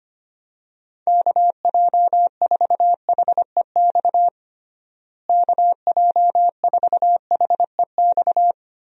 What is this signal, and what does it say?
Morse code, heard as a keyed tone, about 25 wpm, 700 Hz. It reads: KJ4HEX KJ4HEX
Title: USA Call Signs - 6 Characters - Repeated 2x - 25wpm